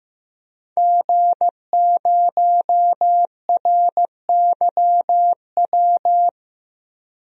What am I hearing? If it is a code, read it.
Morse code, heard as a keyed tone, about 15 wpm, 700 Hz. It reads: G0RYW
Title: DXCC-100 Call Signs - 5 Characters - 15wpm